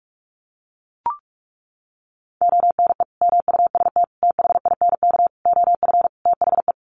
Morse code, heard as a keyed tone, about 44 wpm, 700 Hz. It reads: ONE MUST THINK OF THE
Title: Sentences from Top 700 Words 45wpm